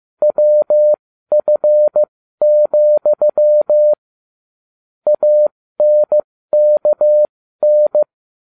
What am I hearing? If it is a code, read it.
Morse code, heard as a keyed tone, about 15 wpm, spaced as for 13 wpm, 600 Hz. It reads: WF, ANKN